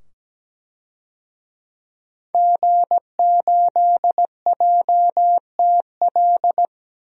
Morse code, heard as a keyed tone, about 17 wpm, 700 Hz. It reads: G8JTL
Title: DXCC-100 Call Signs - 5 Characters - 17wpm